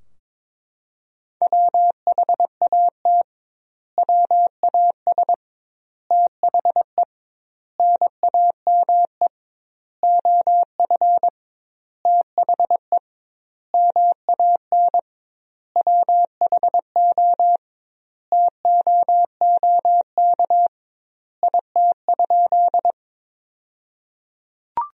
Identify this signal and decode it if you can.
Morse code, heard as a keyed tone, about 22 wpm, 700 Hz. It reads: WHAT WAS THE NAME OF THE MAN WHO TOOK IT?
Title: 2x Word-Spacing - Sentences from Top 300 Words - 22wpm